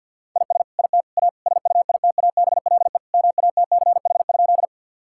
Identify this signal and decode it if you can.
Morse code, heard as a keyed tone, about 50 wpm, 700 Hz. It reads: IS IT A SUITABLE MATCH?
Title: Sentences from Top 1600 Words 50wpm